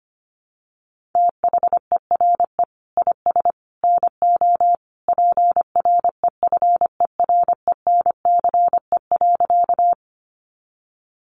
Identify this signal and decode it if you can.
Morse code, heard as a keyed tone, about 25 wpm, 700 Hz. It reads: THERE IS NO PREFERENCE.
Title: Sentences from Top 1500 Words 25wpm